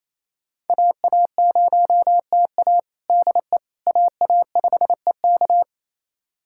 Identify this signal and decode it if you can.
Morse code, heard as a keyed tone, about 28 wpm, 700 Hz. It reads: AA0TA DE AA5EK
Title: USA Callsign DE Callsign - 5 Characters - 28wpm